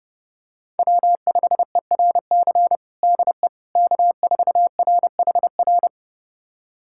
Morse code, heard as a keyed tone, about 31 wpm, 700 Hz. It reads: W5ERC DE K4RHR
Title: USA Callsign DE Callsign 30wpm